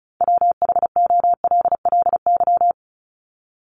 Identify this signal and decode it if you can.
Morse code, heard as a keyed tone, about 34 wpm, 700 Hz. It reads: WHOLLY
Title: LY Suffix - Top 100 Words with Base Words 35wpm